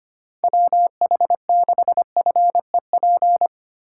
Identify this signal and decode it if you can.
Morse code, heard as a keyed tone, about 25 wpm, 700 Hz. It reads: WH6FEP
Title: USA Call Signs - 6 Characters - 8x Character Spacing - 25wpm